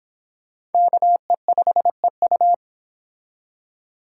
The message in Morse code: KE5EU

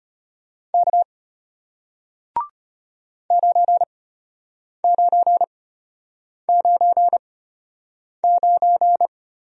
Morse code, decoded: K 9999

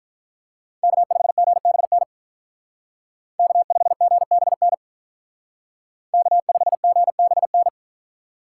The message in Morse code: K5GBN K5GBN K5GBN

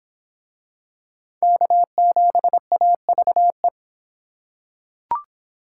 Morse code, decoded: K7AVE

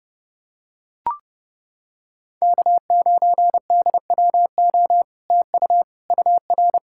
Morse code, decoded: K9DWO TU UR

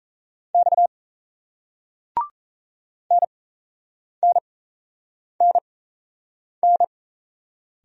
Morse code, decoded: X NNNN

